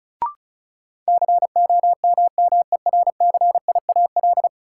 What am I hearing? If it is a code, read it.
Morse code, heard as a keyed tone, about 36 wpm, 700 Hz. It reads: COMMERCIAL
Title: Sentences from Top 1000 Words - Review Part 5 - New Word Drill - 35wpm